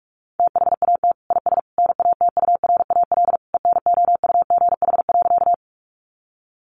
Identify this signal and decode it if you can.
Morse code, heard as a keyed tone, about 44 wpm, 700 Hz. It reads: THAT IS NATURAL ENOUGH.
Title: Sentences from Top 700 Words 45wpm